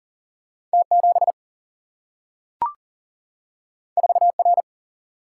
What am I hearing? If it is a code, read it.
Morse code, heard as a keyed tone, about 39 wpm, 700 Hz. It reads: T 7 4 R
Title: Single-Letter Single-Number 40wpm